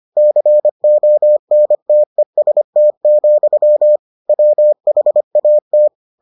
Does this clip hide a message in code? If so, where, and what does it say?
Yes: Morse code from 0.2 to 5.9 s, heard as a keyed tone, about 25 wpm, 600 Hz. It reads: CONTEST, WHAT